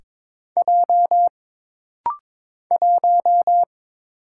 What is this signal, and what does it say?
Morse code, heard as a keyed tone, about 22 wpm, 700 Hz. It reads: J 1
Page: This rapid-fire timing is geared to push your ICR (Instant Character Recognition) skills.